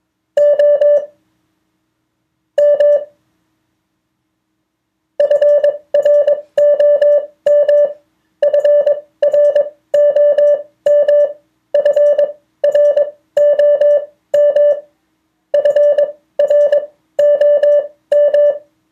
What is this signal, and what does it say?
Morse code, heard as a keyed tone, about 22 wpm, spaced as for 14 wpm, 580 Hz. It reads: OM FROM FROM FROM FROM
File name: 2of4-words-1